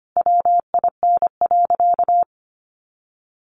Morse code, heard as a keyed tone, about 25 wpm, 700 Hz. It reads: WIN.
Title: Sentences from Top 1500 Words 25wpm